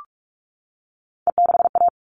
Morse code, heard as a keyed tone, about 44 wpm, 700 Hz. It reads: E6A
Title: All Cut Numbers - 3-Digit 45wpm